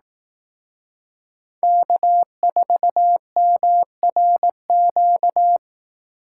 Morse code, heard as a keyed tone, about 18 wpm, 700 Hz. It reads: K4MRQ
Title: Speed Racing - USA 1x3 Call Signs 15wpm